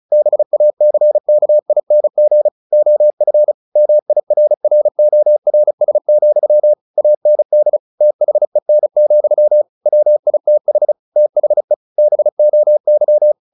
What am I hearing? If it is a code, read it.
Morse code, heard as a keyed tone, about 34 wpm, 600 Hz. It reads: BACKING OF MIRRORS, AND THEN, WITH THE BOY
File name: the old man and the sea / the old man and the sea 1 35wpm